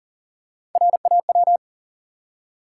Morse code, heard as a keyed tone, about 40 wpm, 700 Hz. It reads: RAW
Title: Three Letter Words - 40wpm